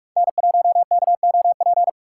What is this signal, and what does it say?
Morse code, heard as a keyed tone, about 46 wpm, 700 Hz. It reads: N1KOP
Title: USA Call Signs - 4x Character Spacing - 45wpm